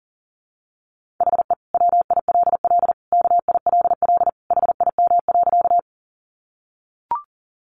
Morse code, heard as a keyed tone, about 40 wpm, 700 Hz. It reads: HE WILL KILL HIM.
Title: Sentences from Top 1400 Words 40wpm